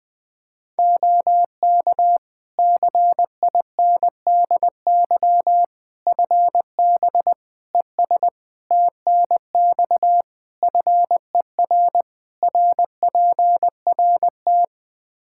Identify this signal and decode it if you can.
Morse code, heard as a keyed tone, about 20 wpm, 700 Hz. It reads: OK CINDY FB ES TNX FER RPRT
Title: K1USN SST QSO Practice 20wpm